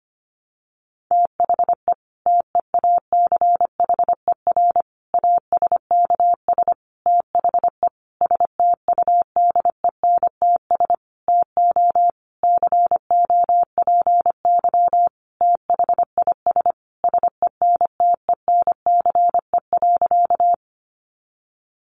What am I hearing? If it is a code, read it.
Morse code, heard as a keyed tone, about 25 wpm, 700 Hz. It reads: THE TEACHER ASKS THE STUDENTS TO COPY THIS SENTENCE.
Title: Additional Easy Phrases 25wpm